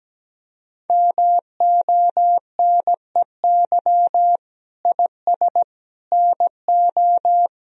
Morse code, heard as a keyed tone, about 17 wpm, 700 Hz. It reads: MONEY IS NO
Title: Sentences from Top 1000 Words - Review Part 3 - 17wpm